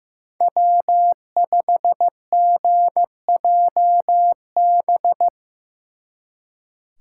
Morse code, heard as a keyed tone, about 18 wpm, 700 Hz. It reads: W5GJB
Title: Speed Racing - USA 1x3 Call Signs 15wpm